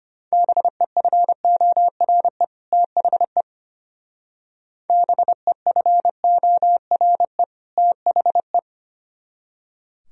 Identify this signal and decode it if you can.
Morse code, heard as a keyed tone, about 32 wpm, 700 Hz. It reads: BEFORE THE BEFORE THE